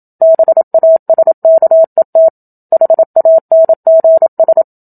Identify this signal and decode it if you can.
Morse code, heard as a keyed tone, about 27 wpm, 650 Hz. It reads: BASKET HANGS
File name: Book Two / 01-Chapter 1 - 27 WPM